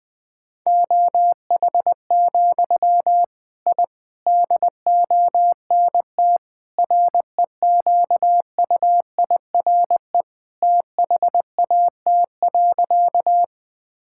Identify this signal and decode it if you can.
Morse code, heard as a keyed tone, about 20 wpm, 700 Hz. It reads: OH, I DONT REQUIRE THAT.
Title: Sentences from Top 1000 Words - Review Part 6 - 20wpm